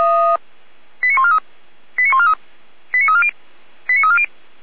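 Five-tone selective call (ZVEI), heard as quick runs of five tones, tones of about 70 ms. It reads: 89143 89143 89340 89340
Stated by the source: ZVEI five-tone call received on an SDR, from Feuerwehr Orlamünde fire brigade.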